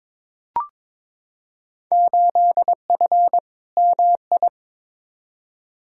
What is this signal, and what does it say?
Morse code, heard as a keyed tone, about 21 wpm, 700 Hz. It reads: 8F MI
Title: Field Day Exchanges Encore - 22wpm